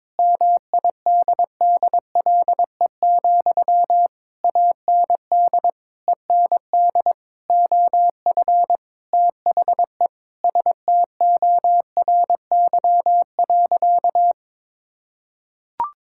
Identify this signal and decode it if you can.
Morse code, heard as a keyed tone, about 22 wpm, 700 Hz. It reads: MIDDLE, AND END OF THE STORY.
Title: Sentences from Top 1000 Words - Review Part 6 - 22wpm